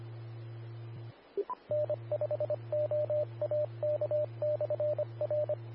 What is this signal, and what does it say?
Morse code, heard as a keyed tone, about 25 wpm, 600 Hz, with radio noise. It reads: N5OAK/R
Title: Austin, TX Amateur Repeaters